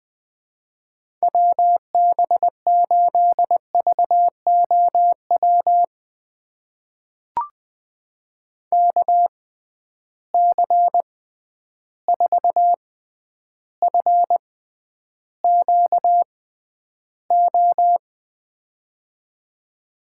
Morse code, heard as a keyed tone, about 20 wpm, spaced as for 10 wpm, 700 Hz. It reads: WB8VOW KC4FQO
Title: USA Call Signs - 6 Characters - 6x Character Spacing - 20wpm